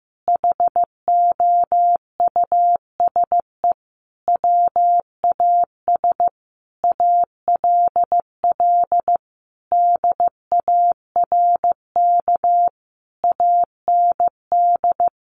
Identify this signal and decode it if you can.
Morse code, heard as a keyed tone, about 15 wpm, 700 Hz. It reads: HOUSE WAS ALL DARK AND
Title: Sentences from Top 500 Words 15wpm